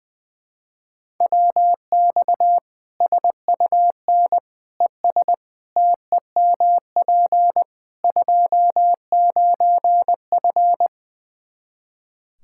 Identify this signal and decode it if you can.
Morse code, heard as a keyed tone, about 21 wpm, 700 Hz. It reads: WX SUN ES TEMP 29F
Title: K1USN SST QSO Practice 20wpm